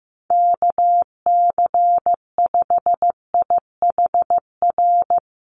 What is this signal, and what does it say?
Morse code, heard as a keyed tone, about 15 wpm, 700 Hz. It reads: KC5IHR
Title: USA Call Signs Repeated 3x 15wpm